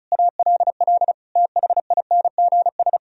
Morse code, heard as a keyed tone, about 46 wpm, 700 Hz. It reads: ALL THINGS